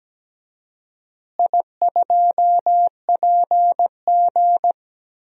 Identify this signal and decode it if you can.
Morse code, heard as a keyed tone, about 17 wpm, 700 Hz. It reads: I2PG
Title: DXCC-100 Call Signs - 4 Characters - Repeated 3x - 17wpm